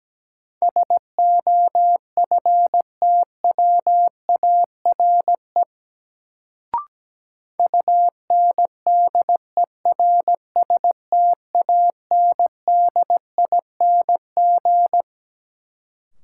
Morse code, heard as a keyed tone, about 17 wpm, 700 Hz. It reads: SOFTWARE UNDERSTANDING
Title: Sentences from Top 1000 Words - Review Part 3 - New Word Drill - 17wpm